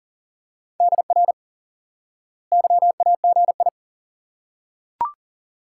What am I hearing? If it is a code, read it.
Morse code, heard as a keyed tone, about 40 wpm, 700 Hz. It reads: DR YAGI
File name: course / instant-qso-element-course-3 / IQE Course 3 - Lesson 89 - YAGI - Two Elements - 40wpm